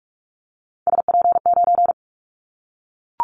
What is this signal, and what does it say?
Morse code, heard as a keyed tone, about 44 wpm, 700 Hz. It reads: SP9
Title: Callsign Prefixes for Top 25 DXCC - 45wpm